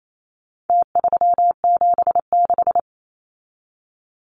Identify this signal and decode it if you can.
Morse code, heard as a keyed tone, about 28 wpm, 700 Hz. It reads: T376
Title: Numbers - Contest Style - 4-Digit 28wpm